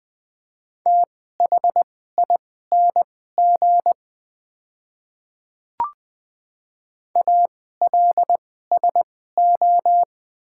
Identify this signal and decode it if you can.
Morse code, heard as a keyed tone, about 21 wpm, spaced as for 15 wpm, 700 Hz. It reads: THING ALSO
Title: 2x Character Spacing - Top 200 Words - 20wpm